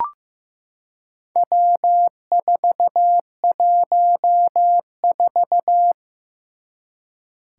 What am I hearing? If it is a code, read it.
Morse code, heard as a keyed tone, about 15 wpm, 700 Hz. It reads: W414
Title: Four Challenging Letter-Number Combinations 15wpm